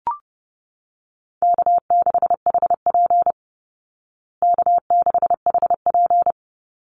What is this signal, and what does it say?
Morse code, heard as a keyed tone, about 30 wpm, 700 Hz. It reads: K6HP K6HP
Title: Easy Call Signs Repeated 2x - 30wpm